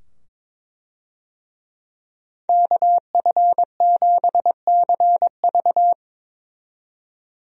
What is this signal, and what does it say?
Morse code, heard as a keyed tone, about 22 wpm, 700 Hz. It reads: KF7CV
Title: USA Call Signs - 5 Characters - 22wpm